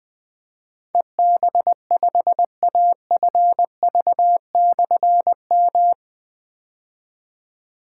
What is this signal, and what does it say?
Morse code, heard as a keyed tone, about 20 wpm, 700 Hz. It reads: EB5AFV/M
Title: Difficult Call Signs 20wpm